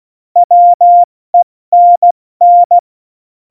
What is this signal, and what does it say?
Morse code, heard as a keyed tone, about 16 wpm, 700 Hz. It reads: WENN
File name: Morsekurs / Witze / Witz 21. Januar 2020, 70 F80